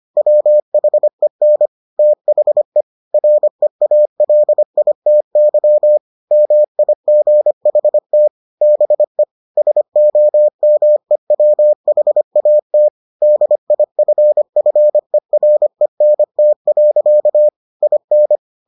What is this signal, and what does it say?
Morse code, heard as a keyed tone, about 25 wpm, 600 Hz. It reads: WHEN THE REALITY MIGHT BE SOMEWHAT DIFFERENT. IN